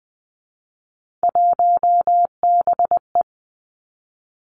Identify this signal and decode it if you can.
Morse code, heard as a keyed tone, about 20 wpm, 700 Hz. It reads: 1BE
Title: Three Challenging Letter-Number Combinations 20wpm